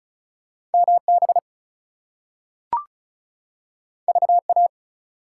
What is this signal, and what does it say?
Morse code, heard as a keyed tone, about 36 wpm, 700 Hz. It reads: MB VA
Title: US States and Canadian Provinces 35wpm